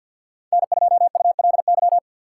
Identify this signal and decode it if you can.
Morse code, heard as a keyed tone, about 46 wpm, 700 Hz. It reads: N2ULY
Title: USA Call Signs - 5 Characters - 3x Character Spacing - 50wpm